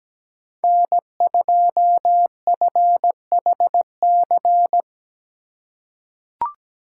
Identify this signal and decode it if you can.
Morse code, heard as a keyed tone, about 17 wpm, 700 Hz. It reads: N2FHC
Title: USA Call Signs - 5 Characters - 4x Character Spacing - 17wpm